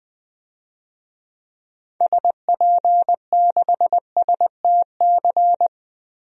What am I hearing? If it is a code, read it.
Morse code, heard as a keyed tone, about 20 wpm, 700 Hz. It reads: SP6STC
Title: DXCC-100 Call Signs - 6 Characters - Repeated 3x - 20wpm